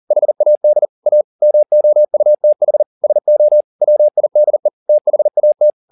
Morse code, heard as a keyed tone, about 40 wpm, 600 Hz. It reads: HAD A MOUTH SO WIDE THAT
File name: the old man and the sea 2 40wpm